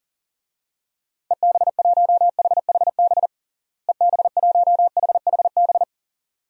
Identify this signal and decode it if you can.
Morse code, heard as a keyed tone, about 40 wpm, 700 Hz. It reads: EB1HHB EB1HHB
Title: DXCC-100 Call Signs - 6 Characters - Repeated 2x - 40wpm